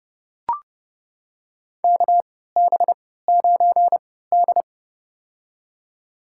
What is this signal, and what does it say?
Morse code, heard as a keyed tone, about 30 wpm, spaced as for 17 wpm, 700 Hz. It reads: KB9D
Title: USA Call Signs - 4 Characters - 3x Character Spacing - 30wpm